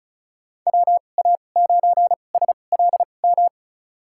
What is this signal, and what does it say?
Morse code, heard as a keyed tone, about 34 wpm, spaced as for 25 wpm, 700 Hz. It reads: WA9SLM
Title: USA Call Signs - 6 Characters - 2x Character Spacing - 35wpm